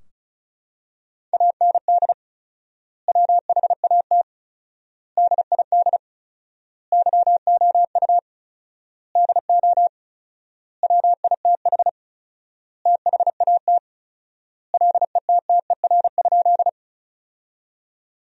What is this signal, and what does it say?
Morse code, heard as a keyed tone, about 36 wpm, 700 Hz. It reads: AND WHAT DID YOU DO WITH THAT LETTER?
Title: 4x Word-Spacing - Sentences from Top 300 Words - 35wpm